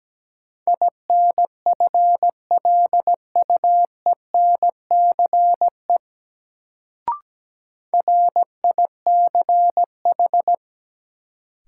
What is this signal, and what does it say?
Morse code, heard as a keyed tone, about 17 wpm, 700 Hz. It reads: INFLUENCE RICH
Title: Sentences from Top 1100 Words - New Word Drill - 17wpm